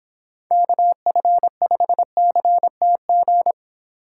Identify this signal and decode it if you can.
Morse code, heard as a keyed tone, about 26 wpm, 700 Hz. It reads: KF5CTG